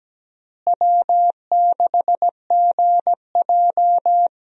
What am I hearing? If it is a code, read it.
Morse code, heard as a keyed tone, about 17 wpm, 700 Hz. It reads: W6GJ
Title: USA Call Signs - 4 Characters - Repeated 2x - 17wpm